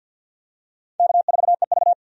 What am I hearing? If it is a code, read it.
Morse code, heard as a keyed tone, about 52 wpm, 700 Hz. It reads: K4EV
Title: USA Call Signs - 4 Characters - 50wpm